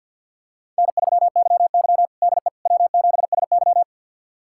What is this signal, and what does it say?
Morse code, heard as a keyed tone, about 51 wpm, 700 Hz. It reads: N3YY DE W7SY
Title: USA Callsign DE Callsign - 4 Characters - 50wpm